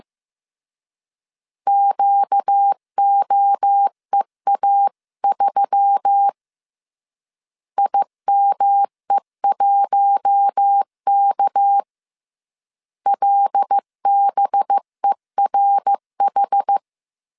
Morse code, heard as a keyed tone, about 15 wpm, 800 Hz. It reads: QOEA3 IME1K LBERH
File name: CW / DET / 1 6 1500C S C 14wpm 2951